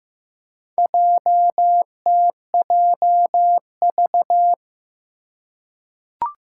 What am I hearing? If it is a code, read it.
Morse code, heard as a keyed tone, about 15 wpm, 700 Hz. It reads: JTJV
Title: Four Challenging Letter-Number Combinations 15wpm